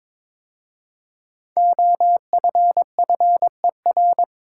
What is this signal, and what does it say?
Morse code, heard as a keyed tone, about 22 wpm, 700 Hz. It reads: OFFER
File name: 5 Letter Words - First 3 Letters Given - 22wpm